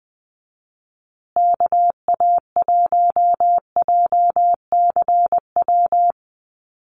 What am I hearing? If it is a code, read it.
Morse code, heard as a keyed tone, about 20 wpm, 700 Hz. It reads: KA1JCW
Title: USA Call Signs Repeated 2x 20wpm